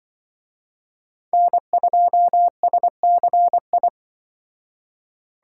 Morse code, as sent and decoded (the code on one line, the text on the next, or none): -. ..--- ... -.-. ..
N2SCI